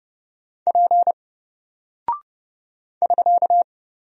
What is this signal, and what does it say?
Morse code, heard as a keyed tone, about 29 wpm, 700 Hz. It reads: P <SK>
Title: Single Character (Letter:Number:Punctuation:Prosign) 30wpm